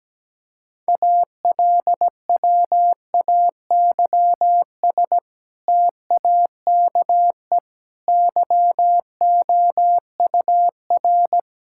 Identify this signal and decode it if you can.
Morse code, heard as a keyed tone, about 17 wpm, 700 Hz. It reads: ALWAYS TAKE YOUR
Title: Sentences from Top 1000 Words - Review Part 5 - 17wpm